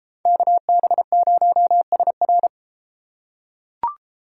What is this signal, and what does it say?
Morse code, heard as a keyed tone, about 32 wpm, 700 Hz. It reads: KB0SR